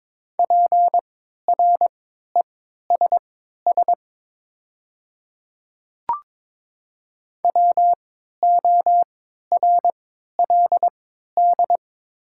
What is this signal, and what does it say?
Morse code, heard as a keyed tone, about 22 wpm, spaced as for 12 wpm, 700 Hz. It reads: PRESS WORLD
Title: 3x Character Spacing - Top 500 Words - 22wpm